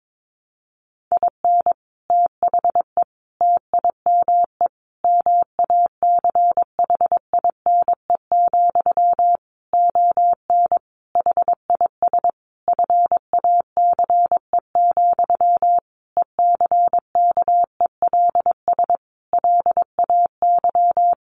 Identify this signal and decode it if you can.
Morse code, heard as a keyed tone, about 22 wpm, 700 Hz. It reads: IN THE TIME MACHINE, ON HIS FACE, ECKELS LAY
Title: A Sound Of Thunder 22wpm